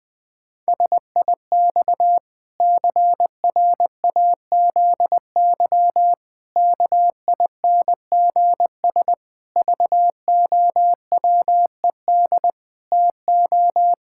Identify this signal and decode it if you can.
Morse code, heard as a keyed tone, about 20 wpm, 700 Hz. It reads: SIX CRAZY KINGS VOWED TO
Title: Pangram Challenge v2 - 20wpm